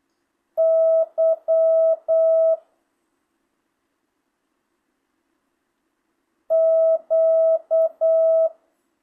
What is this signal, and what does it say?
Morse code, heard as a keyed tone, about 8 wpm, 660 Hz. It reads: Y Q
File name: jh8cbh / 2025cw